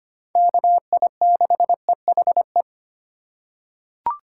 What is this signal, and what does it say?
Morse code, heard as a keyed tone, about 25 wpm, 700 Hz. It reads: KI6EHE
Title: USA Call Signs - 6 Characters - 4x Character Spacing - 25wpm